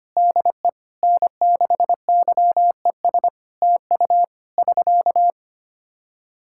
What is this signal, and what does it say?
Morse code, heard as a keyed tone, about 25 wpm, 700 Hz. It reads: DE N6YES TU <SK>
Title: K1USN SST QSO Practice 25wpm